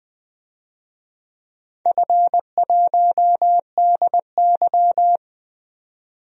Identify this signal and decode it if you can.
Morse code, heard as a keyed tone, about 20 wpm, 700 Hz. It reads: F1DY
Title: DXCC-100 Call Signs - 4 Characters - 20wpm